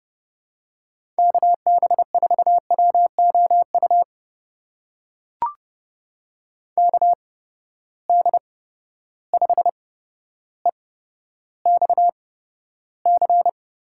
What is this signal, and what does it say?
Morse code, heard as a keyed tone, about 30 wpm, spaced as for 13 wpm, 700 Hz. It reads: KB4WOU KD5EXC
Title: USA Call Signs - 6 Characters - 8x Character Spacing - 30wpm